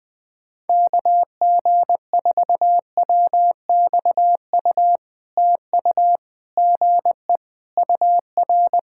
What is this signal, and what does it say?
Morse code, heard as a keyed tone, about 20 wpm, 700 Hz. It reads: KG4WXU TU GE UR
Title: POTA QSO Practice - 20wpm